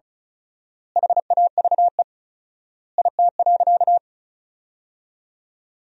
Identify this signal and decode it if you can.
Morse code, heard as a keyed tone, about 36 wpm, 700 Hz. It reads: HAVE IT.
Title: 4x Word-Spacing - Sentences from Top 100 Words - 35wpm